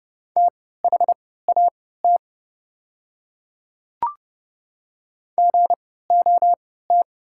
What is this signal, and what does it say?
Morse code, heard as a keyed tone, about 30 wpm, spaced as for 17 wpm, 690 Hz. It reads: THAT GOT
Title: 3x Character Spacing - Top 300 Words - 30wpm